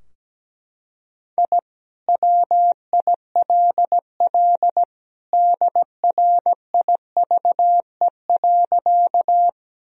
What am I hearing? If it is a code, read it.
Morse code, heard as a keyed tone, about 17 wpm, 700 Hz. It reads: I WILL DRIVE.
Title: Sentences from Top 1000 Words - Review Part 2 - 17wpm